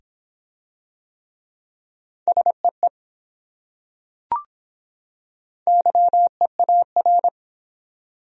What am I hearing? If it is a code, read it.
Morse code, heard as a keyed tone, about 24 wpm, 700 Hz. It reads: SEE YEAR